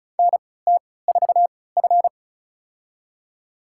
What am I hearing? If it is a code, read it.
Morse code, heard as a keyed tone, about 34 wpm, spaced as for 20 wpm, 700 Hz. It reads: NT4F
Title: USA Call Signs - 4 Characters - 3x Character Spacing - 35wpm